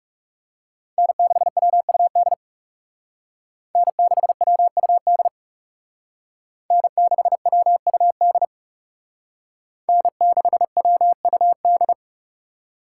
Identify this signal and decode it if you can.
Morse code, heard as a keyed tone, about 38 wpm, 700 Hz. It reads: N6WUD N6WUD N6WUD N6WUD